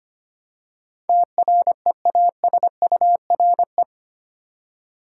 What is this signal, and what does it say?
Morse code, heard as a keyed tone, about 25 wpm, 700 Hz. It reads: TREASURE
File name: Eight Letter Words - 25wpm